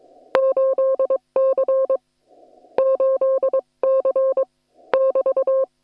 Morse code, heard as a keyed tone, about 22 wpm, 530 Hz. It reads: 8C 8C <BT>
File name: attachments / hf 1 001